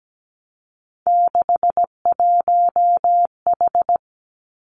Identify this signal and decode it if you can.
Morse code, heard as a keyed tone, about 17 wpm, 700 Hz. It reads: 61H
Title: Three Challenging Letter-Number Combinations 17wpm